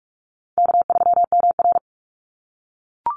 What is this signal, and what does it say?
Morse code, heard as a keyed tone, about 44 wpm, 700 Hz. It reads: K3MR
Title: Call Signs - 45wpm